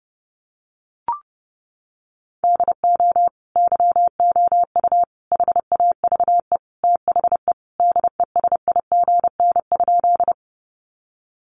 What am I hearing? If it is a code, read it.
Morse code, heard as a keyed tone, about 30 wpm, 700 Hz. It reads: DO YOU HAVE THE DESIGN?
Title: Sentences from Top 800 Words 30wpm